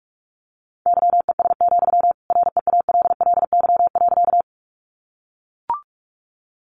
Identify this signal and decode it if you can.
Morse code, heard as a keyed tone, about 44 wpm, 700 Hz. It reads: YES, REALLY.
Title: Sentences from Top 600 Words 45wpm